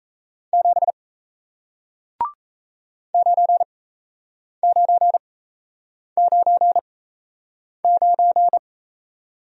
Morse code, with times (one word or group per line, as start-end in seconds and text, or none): 0.5-0.9 7
3.1-8.6 9999